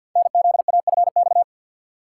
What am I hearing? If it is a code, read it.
Morse code, heard as a keyed tone, about 48 wpm, 690 Hz. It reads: NZAFX
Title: Code Groups - 5 Letters - 50wpm